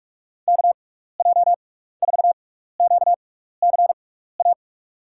Morse code, heard as a keyed tone, about 44 wpm, spaced as for 15 wpm, 700 Hz. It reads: KJ4QCA
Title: USA Call Signs - 6 Characters - 6x Character Spacing - 45wpm